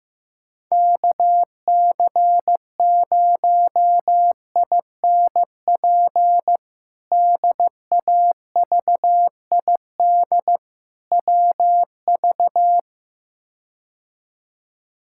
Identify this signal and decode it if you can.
Morse code, heard as a keyed tone, about 15 wpm, 700 Hz. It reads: KC0INP DAVID WV
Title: K1USN SST QSO Practice - 15wpm